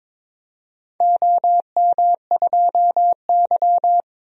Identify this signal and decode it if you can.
Morse code, heard as a keyed tone, about 21 wpm, 700 Hz. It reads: OM2Y
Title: DXCC-100 Call Signs - 4 Characters - Repeated 3x - 22wpm